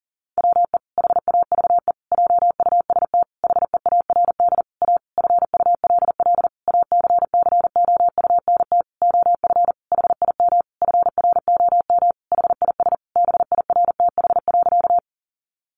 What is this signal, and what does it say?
Morse code, heard as a keyed tone, about 40 wpm, 700 Hz. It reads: WE HAVE JUST HEARD A FULL ACCOUNT OF HIM FROM HIS BIRTH.
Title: Sentences from Top 900 Words 40wpm